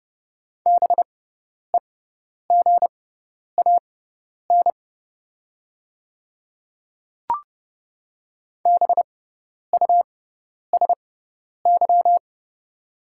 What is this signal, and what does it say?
Morse code, heard as a keyed tone, about 30 wpm, spaced as for 10 wpm, 700 Hz. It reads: BEGAN BUSY
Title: 6x Character Spacing - Top 500 Words - 30wpm